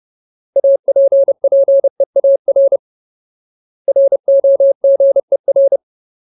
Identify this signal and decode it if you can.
Morse code, heard as a keyed tone, about 30 wpm, 550 Hz. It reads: APPEAR ROGER
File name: Ultimate Word List / 30WPM Ultimate